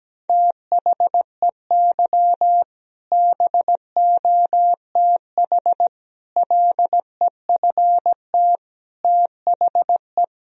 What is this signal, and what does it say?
Morse code, heard as a keyed tone, about 17 wpm, 700 Hz. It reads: THEY BOTH LEFT THE
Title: Sentences from Top 1000 Words - Review Part 1 - 17wpm